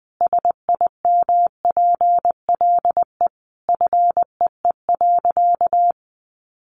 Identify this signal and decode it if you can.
Morse code, heard as a keyed tone, about 20 wpm, 700 Hz. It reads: SIMPLE FEE.
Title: Sentences from Top 1500 Words 20wpm